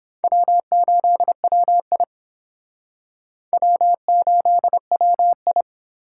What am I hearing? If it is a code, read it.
Morse code, heard as a keyed tone, about 28 wpm, 700 Hz. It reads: W8WI W8WI